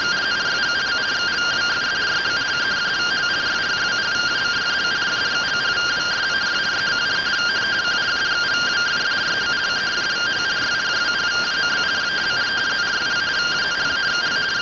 Two-tone teleprinter signal, about 170 Hz shift.